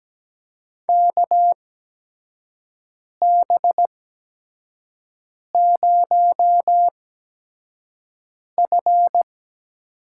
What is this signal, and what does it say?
Morse code, heard as a keyed tone, about 17 wpm, spaced as for 5 wpm, 700 Hz. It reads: KB0F
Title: USA Call Signs - 6 Characters - 8x Character Spacing - 17wpm